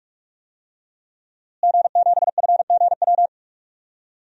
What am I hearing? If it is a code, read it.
Morse code, heard as a keyed tone, about 42 wpm, 700 Hz. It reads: G7FGW
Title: DXCC-100 Call Signs - 5 Characters - Repeated 2x - 45wpm